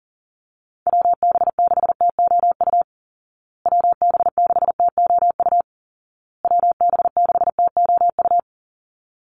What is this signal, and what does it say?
Morse code, heard as a keyed tone, about 40 wpm, 700 Hz. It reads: WB6TOU WB6TOU WB6TOU
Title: Call Signs Repeated 2x - 40wpm